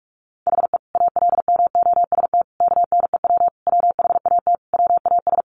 Morse code, heard as a keyed tone, about 44 wpm, 700 Hz. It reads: HE ALMOST KNEW WHAT WAS
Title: Sentences from Top 700 Words 45wpm